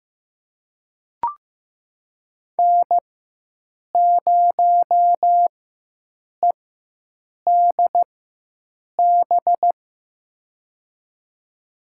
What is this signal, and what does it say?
Morse code, heard as a keyed tone, about 15 wpm, spaced as for 7 wpm, 700 Hz. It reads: N0EDB
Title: USA Call Signs - 5 Characters - 4x Character Spacing - 15wpm